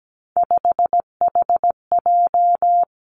Morse code, heard as a keyed tone, about 17 wpm, 700 Hz. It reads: 5HJ
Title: Three Challenging Letter-Number Combinations 17wpm